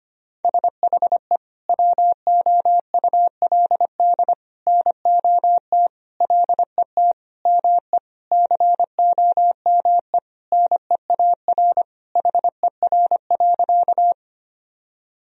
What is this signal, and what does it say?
Morse code, heard as a keyed tone, about 25 wpm, 700 Hz. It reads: SHE WOULD NOT LET ME COME NEAR HER.
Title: Sentences from Top 1000 Words - Review Part 1 - 25wpm